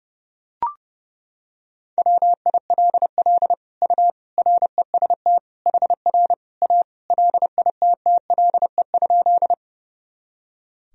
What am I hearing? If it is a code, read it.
Morse code, heard as a keyed tone, about 30 wpm, 700 Hz. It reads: WILL U REST HR A LITTLE?
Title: Sentences from Top 500 Words and Abbreviations 30wpm